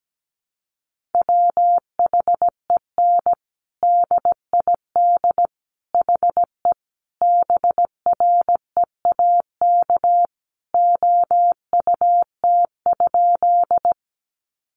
Morse code, heard as a keyed tone, about 17 wpm, 700 Hz. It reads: WHEN DID HE BREAK OUT?
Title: Sentences from Top 900 Words 17wpm